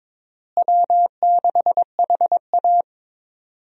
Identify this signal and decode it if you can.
Morse code, heard as a keyed tone, about 22 wpm, 700 Hz. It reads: W6HA
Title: USA Call Signs - 4 Characters - Repeated 3x - 22wpm